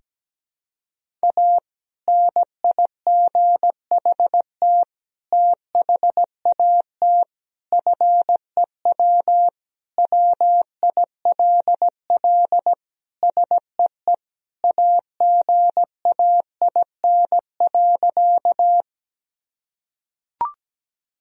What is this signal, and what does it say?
Morse code, heard as a keyed tone, about 17 wpm, 700 Hz. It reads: A NIGHT THAT FEW WILL SEE AGAIN.
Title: Sentences from Top 1000 Words - Review Part 1 - 17wpm